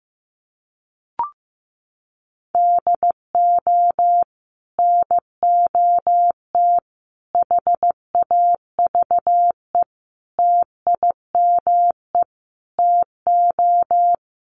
Morse code, heard as a keyed tone, about 15 wpm, 700 Hz. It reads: DO NOT HAVE TIME TO
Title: Sets of 5 Words 15wpm